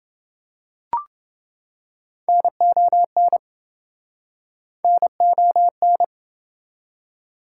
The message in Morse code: NON NON

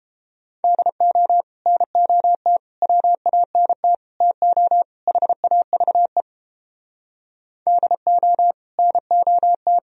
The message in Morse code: DO NOT WANT TO HAVE DO NOT

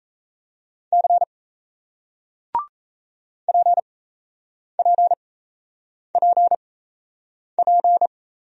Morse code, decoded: C PPPP

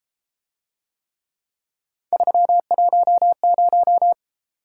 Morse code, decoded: 310

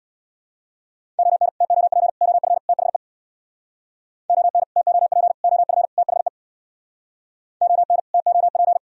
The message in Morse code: ON TOP OF THE ON TOP OF THE ON TOP

